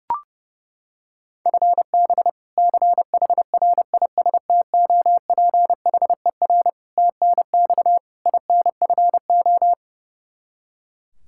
Morse code, heard as a keyed tone, about 30 wpm, 700 Hz. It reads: FB CHRISTOPHER TNX INFO
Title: K1USN SST QSO Practice 30wpm